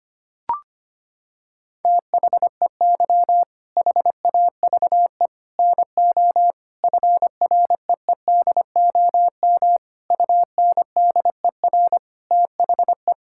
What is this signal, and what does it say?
Morse code, heard as a keyed tone, about 25 wpm, 700 Hz. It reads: THEY HAVE NO FREEDOM UNDER THE
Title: Sentences from Top 1000 Words - Review Part 5 - 25wpm